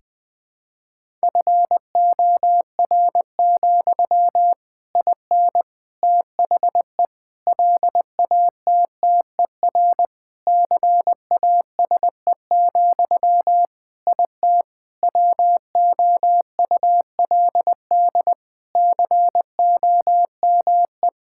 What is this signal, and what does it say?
Morse code, heard as a keyed tone, about 20 wpm, 700 Hz. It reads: FOR, IN THE LATTER CASE, IT WOULD COME
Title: Sentences from Top 1600 Words 20wpm